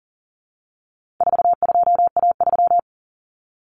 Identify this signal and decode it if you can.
Morse code, heard as a keyed tone, about 40 wpm, 700 Hz. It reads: 42A3
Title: Common Cut Numbers - 4-Digit 40wpm